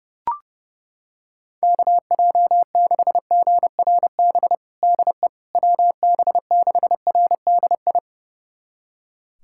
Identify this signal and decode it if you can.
Morse code, heard as a keyed tone, about 31 wpm, 700 Hz. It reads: KJ6GRB DE WB6RDI
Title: USA Callsign DE Callsign - 6 Characters - 30wpm